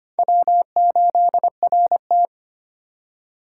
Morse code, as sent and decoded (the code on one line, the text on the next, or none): .-- ---.. .-. -
W8RT